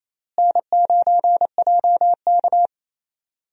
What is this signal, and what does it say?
Morse code, heard as a keyed tone, about 28 wpm, 700 Hz. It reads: N9JK
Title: USA Call Signs - 4 Characters - Repeated 2x - 28wpm